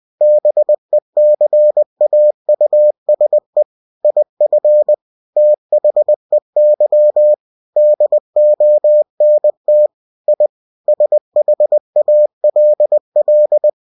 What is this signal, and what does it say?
Morse code, heard as a keyed tone, about 20 wpm, 600 Hz. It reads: BECAUSE IF THEY DONT I SHALL
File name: cw-books / winnie-the-pooh / winnie-20wpm-page-0093